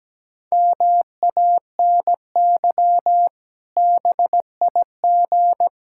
Morse code, heard as a keyed tone, about 17 wpm, 700 Hz. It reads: MANY BIG
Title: Pangram Challenge v2 - 17wpm